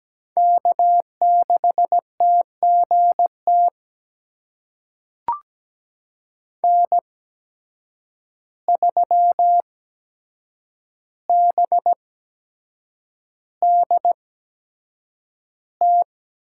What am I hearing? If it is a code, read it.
Morse code, heard as a keyed tone, about 17 wpm, spaced as for 7 wpm, 700 Hz. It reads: K6TGT N3BDT
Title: USA Call Signs - 5 Characters - 8x Character Spacing - 17wpm